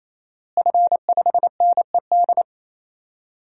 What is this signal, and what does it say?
Morse code, heard as a keyed tone, about 28 wpm, 700 Hz. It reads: F5NED
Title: DXCC-100 Call Signs - 5 Characters - Repeated 3x - 28wpm